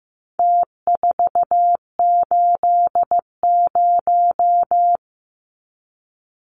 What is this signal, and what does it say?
Morse code, heard as a keyed tone, about 15 wpm, 700 Hz. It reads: T480
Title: Numbers - Contest Style - 4-Digit 15wpm